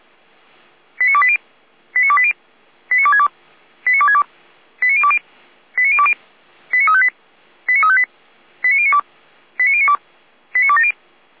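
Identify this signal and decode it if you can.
Five-tone selective call (ZVEI), heard as quick runs of five tones, tones of about 70 ms. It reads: 89280 89280 89262 89262 89020 89020 89358 89358 89092 89092 89270
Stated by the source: Five-tone ZVEI selective call received on an SDR by feuerwehr-orlamuende (fire brigade Orlamünde).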